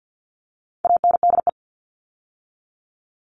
Morse code, heard as a keyed tone, about 50 wpm, 700 Hz. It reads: ANDE
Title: All Cut Numbers - 4-Digit 50wpm